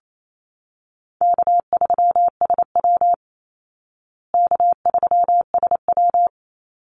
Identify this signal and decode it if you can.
Morse code, heard as a keyed tone, about 28 wpm, 700 Hz. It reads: K3SW K3SW
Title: Easy Call Signs Repeated 2x - 28wpm